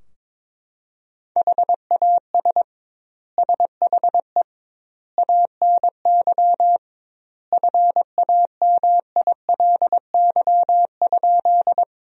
Morse code, heard as a keyed tone, about 22 wpm, 700 Hz. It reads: HAS SHE ANY FAMILY?
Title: 2x Word-Spacing - Sentences from Top 400 Words - 22wpm